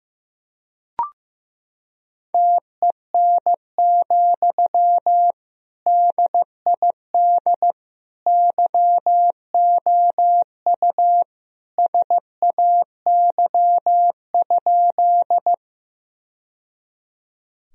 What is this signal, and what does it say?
Morse code, heard as a keyed tone, about 15 wpm, 700 Hz. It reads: TEN, DID YOU SAY?
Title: Sentences from Top 1000 Words - Review Part 2 - 15wpm